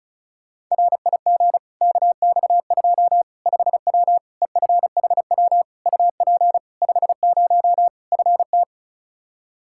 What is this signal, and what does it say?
Morse code, heard as a keyed tone, about 34 wpm, 700 Hz. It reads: RIG KX2 5W EFHW UP 50 FT
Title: Practice QSO - Exchange 2 - Info - 35wpm